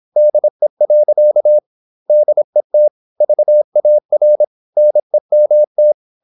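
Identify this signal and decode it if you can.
Morse code, heard as a keyed tone, about 26 wpm, 600 Hz. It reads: DE. DET VAR NEMT
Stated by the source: High-speed (QRQ) Morse at 26 wpm.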